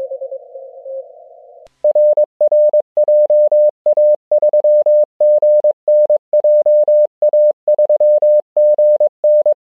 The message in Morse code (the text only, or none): RR JA3GN JA3GN